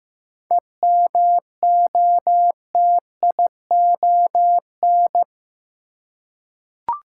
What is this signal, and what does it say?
Morse code, heard as a keyed tone, about 15 wpm, 700 Hz. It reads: EMOTION
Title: Seven Letter Words - 15wpm